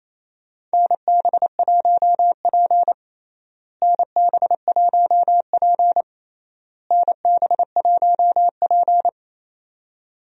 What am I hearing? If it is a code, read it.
Morse code, heard as a keyed tone, about 28 wpm, 700 Hz. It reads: NB1P NB1P NB1P
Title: USA Call Signs - 4 Characters - Repeated 3x - 28wpm